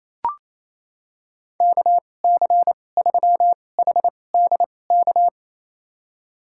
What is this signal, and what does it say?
Morse code, heard as a keyed tone, about 28 wpm, spaced as for 20 wpm, 700 Hz. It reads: KC3HDK
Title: USA Call Signs - 6 Characters - 2x Character Spacing - 28wpm